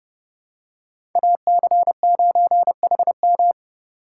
Morse code, heard as a keyed tone, about 30 wpm, 700 Hz. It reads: AC9HM
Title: USA Call Signs - 5 Characters - 2x Character Spacing - 30wpm